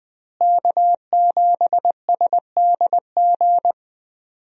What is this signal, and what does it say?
Morse code, heard as a keyed tone, about 20 wpm, 700 Hz. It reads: K7SDG
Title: USA Call Signs - 5 Characters - Repeated 3x - 20wpm